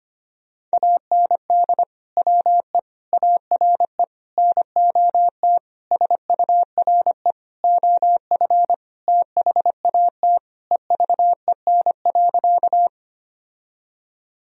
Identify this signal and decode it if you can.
Morse code, heard as a keyed tone, about 25 wpm, 700 Hz. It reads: AND WE ARE NOT SURE OF THAT EVEN.
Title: Sentences from Top 1000 Words - Review Part 1 - 25wpm